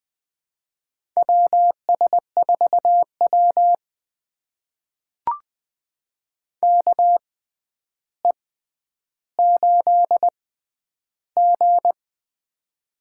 Morse code, heard as a keyed tone, about 20 wpm, spaced as for 10 wpm, 700 Hz. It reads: WS4W KE8G
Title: USA Call Signs - 4 Characters - 6x Character Spacing - 20wpm